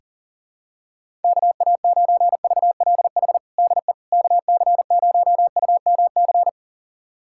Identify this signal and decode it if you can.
Morse code, heard as a keyed tone, about 40 wpm, 700 Hz. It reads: KA9VLH DE KC0UMC
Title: USA Callsign DE Callsign - 6 Characters - 40wpm